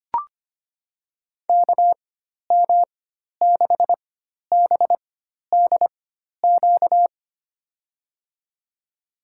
Morse code, heard as a keyed tone, about 25 wpm, spaced as for 12 wpm, 700 Hz. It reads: KM6BDQ
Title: USA Call Signs - 4x Character Spacing - 25wpm